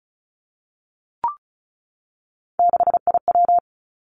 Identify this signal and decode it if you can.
Morse code, heard as a keyed tone, about 36 wpm, 700 Hz. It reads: 6IW
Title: Three Challenging Letter-Number Combinations 35wpm